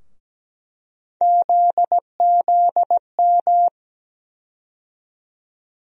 Morse code, heard as a keyed tone, about 17 wpm, 700 Hz. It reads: ZZM
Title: USA Call Sign Postfixes - 3 Letters 17wpm